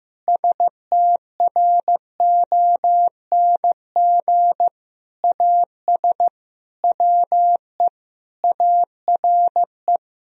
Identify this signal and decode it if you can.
Morse code, heard as a keyed tone, about 16 wpm, 700 Hz. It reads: STRONG AS WE ARE
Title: Sentences from Top 1000 Words - Review Part 4 - 15wpm